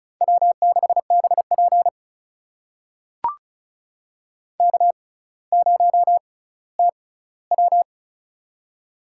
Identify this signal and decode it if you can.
Morse code, heard as a keyed tone, about 36 wpm, spaced as for 18 wpm, 700 Hz. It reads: W6BP K0TW
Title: USA Call Signs - 4 Characters - 6x Character Spacing - 35wpm